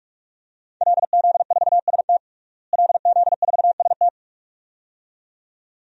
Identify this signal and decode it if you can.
Morse code, heard as a keyed tone, about 44 wpm, 700 Hz. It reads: LZ4ST LZ4ST
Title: DXCC-100 Call Signs - 5 Characters - Repeated 2x - 45wpm